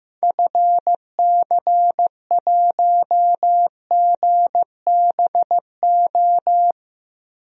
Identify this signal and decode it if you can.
Morse code, heard as a keyed tone, about 15 wpm, 700 Hz. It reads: FC1GBO
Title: DXCC-100 Call Signs - 6 Characters - Repeated 2x - 15wpm